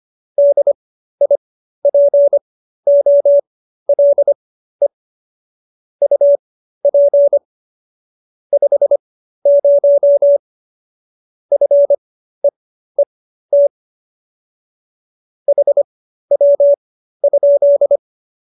Morse code, heard as a keyed tone, about 25 wpm, spaced as for 13 wpm, 570 Hz. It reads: DIPOLE UP 50 FEET HW?